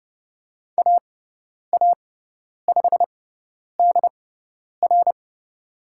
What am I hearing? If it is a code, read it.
Morse code, heard as a keyed tone, about 30 wpm, spaced as for 10 wpm, 700 Hz. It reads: AA5DR
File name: Lesson 18g - Letter m - Call Signs 30wpm